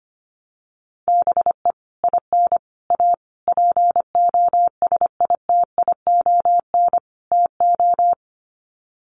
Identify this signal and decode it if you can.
Morse code, heard as a keyed tone, about 25 wpm, 700 Hz. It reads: BE IN A POSITION TO
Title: Sets of 5 Words Encore2 25wpm